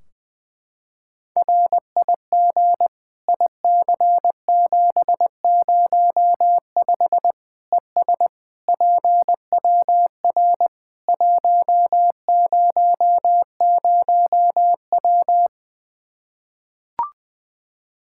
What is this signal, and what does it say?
Morse code, heard as a keyed tone, about 20 wpm, 700 Hz. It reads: RIG IC705 ES PWR 100W
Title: Practice QSO - Exchange 2 - Info - 20wpm